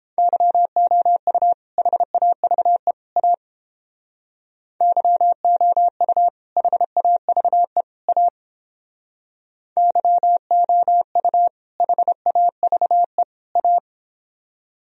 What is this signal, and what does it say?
Morse code, heard as a keyed tone, about 31 wpm, 700 Hz. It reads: YOU HAVE A YOU HAVE A YOU HAVE A